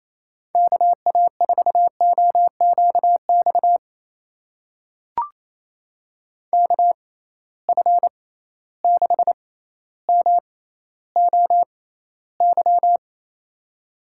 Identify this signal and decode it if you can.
Morse code, heard as a keyed tone, about 28 wpm, spaced as for 14 wpm, 700 Hz. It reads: KA4OQX KF6MOY
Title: USA Call Signs - 6 Characters - 6x Character Spacing - 28wpm